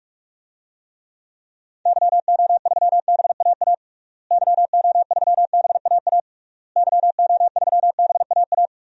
Morse code, heard as a keyed tone, about 46 wpm, 700 Hz. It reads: YO3BAA YO3BAA YO3BAA
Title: DXCC-100 Call Signs - 6 Characters - Repeated 3x - 45wpm